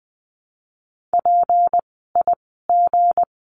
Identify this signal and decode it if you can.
Morse code, heard as a keyed tone, about 20 wpm, spaced as for 15 wpm, 700 Hz. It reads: PIG
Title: 160 Lesson 26f - Letter g - Words 20wpm